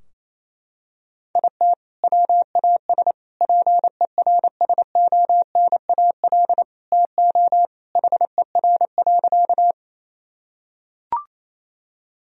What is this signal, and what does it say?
Morse code, heard as a keyed tone, about 28 wpm, 700 Hz. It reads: IT WAS PERSONAL TO HER.
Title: Sentences from Top 1000 Words - Review Part 4 - 28wpm